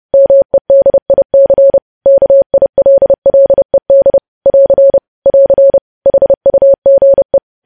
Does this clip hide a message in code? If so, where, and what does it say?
Yes: Morse code from 0.1 to 7.4 s, heard as a keyed tone, about 30 wpm, 550 Hz. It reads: MEDIC KILLED <AR> <AR> HUGE